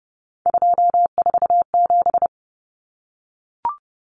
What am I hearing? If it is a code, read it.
Morse code, heard as a keyed tone, about 30 wpm, 700 Hz. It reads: 247
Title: Numbers - 3-Digit 30wpm